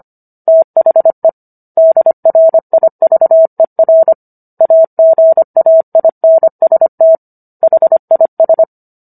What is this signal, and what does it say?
Morse code, heard as a keyed tone, about 25 wpm, 650 Hz. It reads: THE DRIVER AGAINST HIS